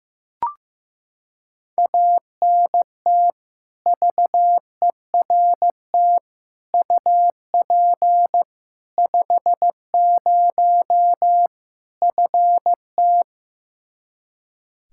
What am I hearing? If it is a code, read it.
Morse code, heard as a keyed tone, about 15 wpm, 700 Hz. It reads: ANT VERT UP 50 FT
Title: Practice QSO - Exchange 2 - Info - 15wpm